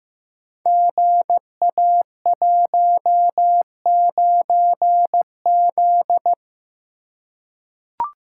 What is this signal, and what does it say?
Morse code, heard as a keyed tone, about 15 wpm, 700 Hz. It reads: GA19Z
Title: Code Groups - 5 Letter:Numbers - 15wpm